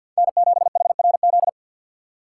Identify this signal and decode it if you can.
Morse code, heard as a keyed tone, about 52 wpm, 700 Hz. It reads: N7HRZ
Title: USA Call Signs - 2x Character Spacing - 50wpm